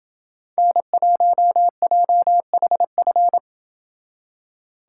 Morse code, heard as a keyed tone, about 28 wpm, 700 Hz. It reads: N1JHF